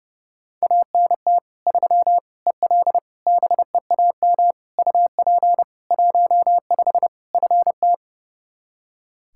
Morse code, heard as a keyed tone, about 32 wpm, 700 Hz. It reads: ANT 3 EL BEAM UP 15 FT
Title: Practice QSO - Exchange 2 - Info - 30wpm